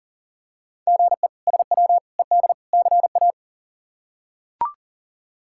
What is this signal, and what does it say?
Morse code, heard as a keyed tone, about 40 wpm, 700 Hz. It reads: GE SW ED CA
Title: K1USN SST QSO Practice - 40wpm